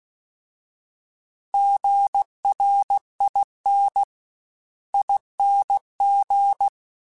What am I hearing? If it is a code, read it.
Morse code, heard as a keyed tone, about 16 wpm, 800 Hz. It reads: GRIN ING